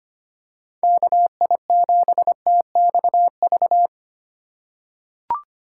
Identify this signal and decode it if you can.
Morse code, heard as a keyed tone, about 25 wpm, 700 Hz. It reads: KI7TXV
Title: USA Call Signs - 6 Characters - 2x Character Spacing - 25wpm